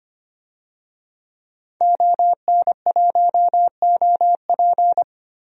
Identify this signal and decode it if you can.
Morse code, heard as a keyed tone, about 25 wpm, 700 Hz. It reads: ON1OP
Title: DXCC-100 Call Signs - 5 Characters - 25wpm